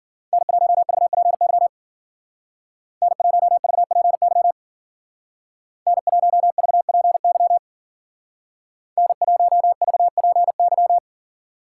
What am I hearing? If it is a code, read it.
Morse code, heard as a keyed tone, about 50 wpm, 700 Hz. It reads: N1VPY N1VPY N1VPY N1VPY